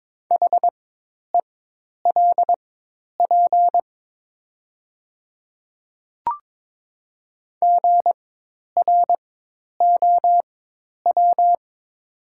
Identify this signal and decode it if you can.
Morse code, heard as a keyed tone, about 22 wpm, spaced as for 10 wpm, 700 Hz. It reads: HELP GROW
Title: 4x Character Spacing - Top 300 Words - 22wpm